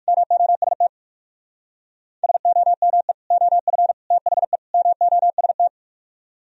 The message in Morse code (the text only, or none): MOST SOME OF THE MOST